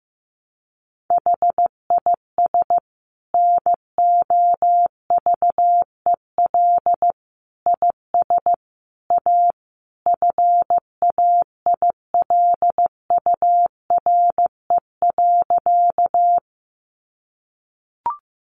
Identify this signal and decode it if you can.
Morse code, heard as a keyed tone, about 15 wpm, 700 Hz. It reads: HIS NOVEL IS A FAILURE.
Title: Sentences from Top 1300 Words 15wpm